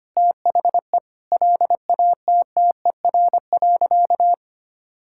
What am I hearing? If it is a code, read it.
Morse code, heard as a keyed tone, about 25 wpm, 700 Hz. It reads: THE LATTER.
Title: Sentences from Top 1600 Words 25wpm